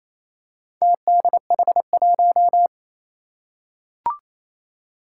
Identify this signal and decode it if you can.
Morse code, heard as a keyed tone, about 28 wpm, 700 Hz. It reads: TDH1
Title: Four Challenging Letter-Number Combinations 28wpm